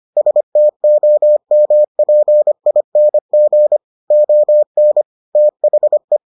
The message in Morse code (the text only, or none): STOMPING ON THE